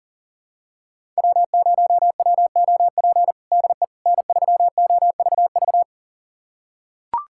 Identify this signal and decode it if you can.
Morse code, heard as a keyed tone, about 37 wpm, 700 Hz. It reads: W0WOP DE N3OVV
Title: USA Callsign DE Callsign - 5 Characters - 40wpm